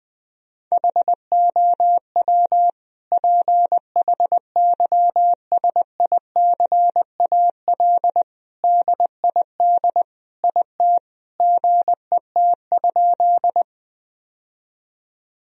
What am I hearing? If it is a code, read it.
Morse code, heard as a keyed tone, about 21 wpm, 700 Hz. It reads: HOW PHYSICAL DID IT GET?
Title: Sentences from Top 1000 Words - Review Part 4 - 20wpm